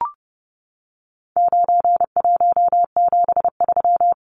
Morse code, heard as a keyed tone, about 29 wpm, 700 Hz. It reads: 9173
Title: Numbers - 4-Digit 30wpm